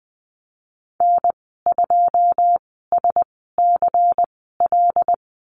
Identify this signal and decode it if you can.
Morse code, heard as a keyed tone, about 19 wpm, spaced as for 14 wpm, 700 Hz. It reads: N2SCL
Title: 147 Lesson 24g - Letter y - Call Signs 20wpm